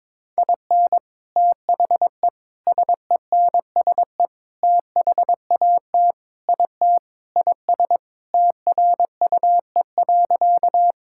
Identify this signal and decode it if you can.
Morse code, heard as a keyed tone, about 22 wpm, 700 Hz. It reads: IN THE SENSE THAT IT IS TRUE.
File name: Sentences with 4-Word Phrases - 22wpm